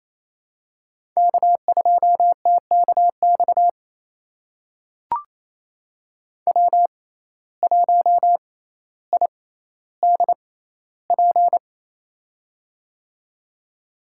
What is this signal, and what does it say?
Morse code, heard as a keyed tone, about 28 wpm, spaced as for 14 wpm, 700 Hz. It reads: K2TKX W1IDP
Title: USA Call Signs - 5 Characters - 6x Character Spacing - 28wpm